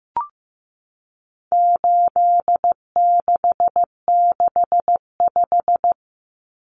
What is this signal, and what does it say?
Morse code, heard as a keyed tone, about 14 wpm, 700 Hz. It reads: 8665
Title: Common Cut Numbers - 4-Digit 15wpm